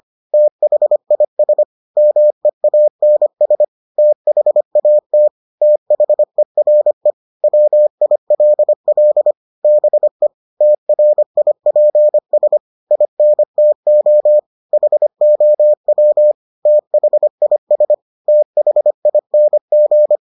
Morse code, heard as a keyed tone, about 25 wpm, 600 Hz. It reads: THIS MEANS THAT THERE WILL BE TRIPS INTO HOW THIS THING